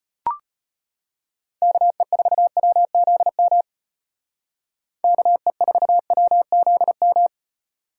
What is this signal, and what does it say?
Morse code, heard as a keyed tone, about 35 wpm, 700 Hz. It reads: KE4WZM KE4WZM